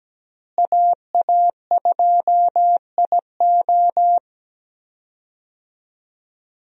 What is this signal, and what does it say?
Morse code, heard as a keyed tone, about 17 wpm, 700 Hz. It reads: AA2IO
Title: USA Call Signs - 5 Characters - Repeated 3x - 17wpm